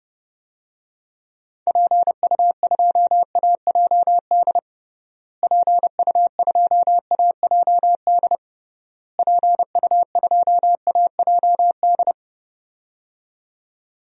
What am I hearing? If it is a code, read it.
Morse code, heard as a keyed tone, about 30 wpm, 700 Hz. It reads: PU2AJD PU2AJD PU2AJD
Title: DXCC-100 Call Signs - 6 Characters - Repeated 3x - 30wpm